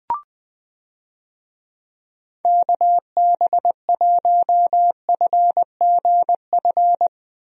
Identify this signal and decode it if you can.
Morse code, heard as a keyed tone, about 20 wpm, 700 Hz. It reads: KB1FGF
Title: USA Call Signs - 6 Characters - Repeated 2x - 20wpm